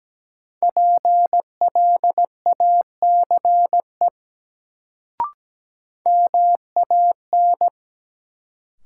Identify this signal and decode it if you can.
Morse code, heard as a keyed tone, about 17 wpm, 700 Hz. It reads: PLACE MAN
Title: Sentences from Top 200 Words - New Word Drill - 17wpm